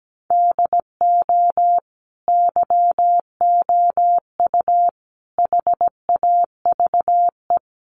Morse code, heard as a keyed tone, about 17 wpm, 700 Hz. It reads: DO YOU HAVE
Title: Sentences from Top 1200 Words 17wpm